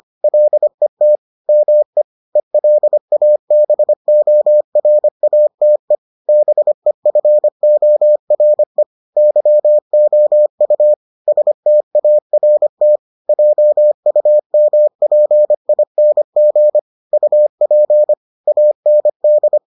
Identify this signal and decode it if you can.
Morse code, heard as a keyed tone, about 25 wpm, 600 Hz. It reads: LET ME ELABORATE BEFORE YOU START JUMPING UP AND